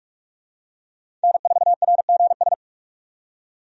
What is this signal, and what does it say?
Morse code, heard as a keyed tone, about 46 wpm, 700 Hz. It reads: N4RGS
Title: USA Call Signs - 5 Characters - 45wpm